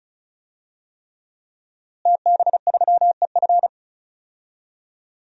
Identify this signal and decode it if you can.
Morse code, heard as a keyed tone, about 36 wpm, 700 Hz. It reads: TB3EF
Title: DXCC-100 Call Signs - 5 Characters - Repeated 3x - 35wpm